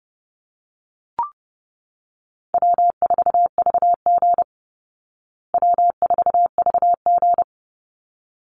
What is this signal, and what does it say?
Morse code, heard as a keyed tone, about 30 wpm, 700 Hz. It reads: W4VG W4VG
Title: Call Signs Repeated 3x - 30wpm